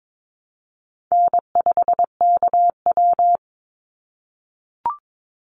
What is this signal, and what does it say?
Morse code, heard as a keyed tone, about 22 wpm, 700 Hz. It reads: N5KW
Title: Easy Call Signs - 22wpm